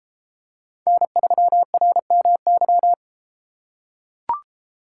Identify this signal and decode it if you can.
Morse code, heard as a keyed tone, about 32 wpm, 700 Hz. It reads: N3RMY